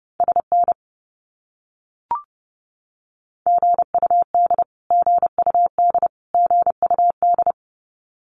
Morse code, heard as a keyed tone, about 30 wpm, 700 Hz. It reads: SN GUD GUD GUD
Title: Abbreviations Repeated 3x - 30wpm